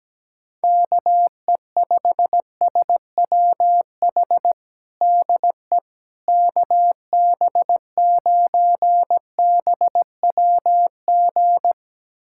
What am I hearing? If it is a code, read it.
Morse code, heard as a keyed tone, about 17 wpm, 700 Hz. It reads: KE5SWH DE KB9BWG
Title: USA Callsign DE Callsign - 6 Characters - 17wpm